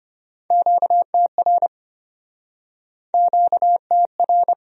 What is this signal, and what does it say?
Morse code, heard as a keyed tone, about 35 wpm, 700 Hz. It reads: QTR QTR